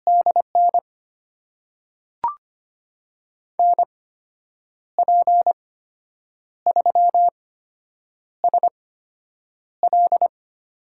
Morse code, heard as a keyed tone, about 25 wpm, spaced as for 8 wpm, 700 Hz. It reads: DN NP3SL
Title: USA Call Signs - 5 Characters - 8x Character Spacing - 25wpm